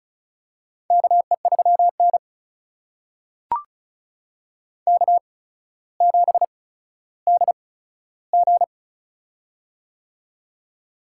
Morse code, heard as a keyed tone, about 34 wpm, spaced as for 15 wpm, 700 Hz. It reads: KE3N K7DG
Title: USA Call Signs - 4 Characters - 8x Character Spacing - 35wpm